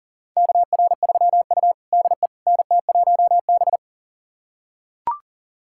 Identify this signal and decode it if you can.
Morse code, heard as a keyed tone, about 40 wpm, 700 Hz. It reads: KR3U DE NT1B
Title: USA Callsign DE Callsign - 4 Characters - 40wpm